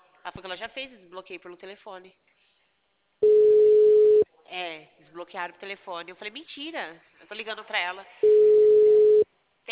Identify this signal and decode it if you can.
Ringback tone (Europe, 425 Hz), with speech.